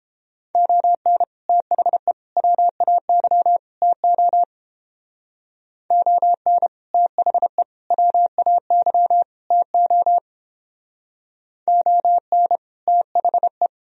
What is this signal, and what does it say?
Morse code, heard as a keyed tone, about 29 wpm, 700 Hz. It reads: ON THE WAY TO ON THE WAY TO ON THE